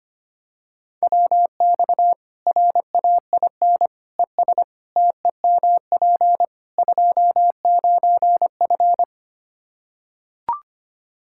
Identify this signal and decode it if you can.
Morse code, heard as a keyed tone, about 25 wpm, 700 Hz. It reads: WX RAIN ES TEMP 29F
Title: Practice QSO - Exchange 2 - Info - 25wpm